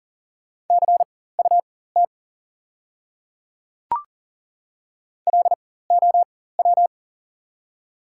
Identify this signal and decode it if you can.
Morse code, heard as a keyed tone, about 40 wpm, spaced as for 19 wpm, 690 Hz. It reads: CUT LOW
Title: 4x Character Spacing - Top 300 Words - 40wpm